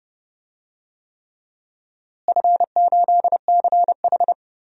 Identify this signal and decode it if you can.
Morse code, heard as a keyed tone, about 30 wpm, 700 Hz. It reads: F8CH
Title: DXCC-100 Call Signs - 4 Characters - Repeated 3x - 30wpm